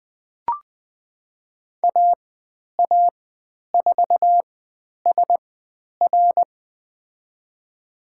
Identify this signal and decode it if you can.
Morse code, heard as a keyed tone, about 20 wpm, spaced as for 10 wpm, 700 Hz. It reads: AA4SR
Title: 095 Lesson 16g - Letter c - Call Signs 20wpm